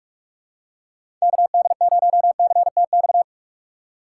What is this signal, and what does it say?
Morse code, heard as a keyed tone, about 44 wpm, 700 Hz. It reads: KD0CTX